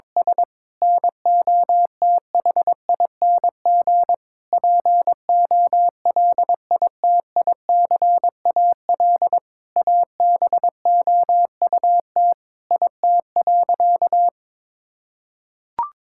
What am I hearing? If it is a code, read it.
Morse code, heard as a keyed tone, about 22 wpm, 700 Hz. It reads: S NOTHING POLITICAL ABOUT IT.
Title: Sentences from Top 1000 Words - Review Part 5 - 22wpm